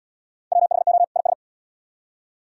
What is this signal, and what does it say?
Morse code, heard as a keyed tone, about 69 wpm, 700 Hz. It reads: WHO IS